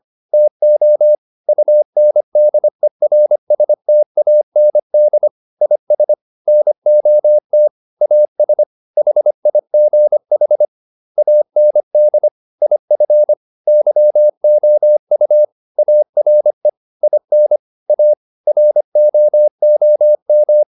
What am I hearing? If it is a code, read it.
Morse code, heard as a keyed tone, about 25 wpm, 600 Hz. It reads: TO UNDERSTAND IS NOT AS HIGH AND IF YOU ARE IN A ROOM